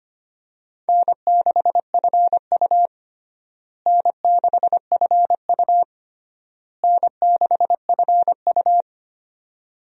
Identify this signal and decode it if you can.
Morse code, heard as a keyed tone, about 25 wpm, 700 Hz. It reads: N6FU N6FU N6FU
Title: USA Call Signs - 4 Characters - Repeated 3x - 25wpm